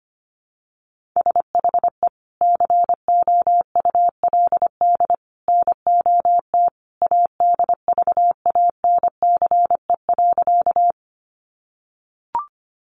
Morse code, heard as a keyed tone, about 25 wpm, 700 Hz. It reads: SHE COULD NOT ADVANCE.
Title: Sentences from Top 1400 Words 25wpm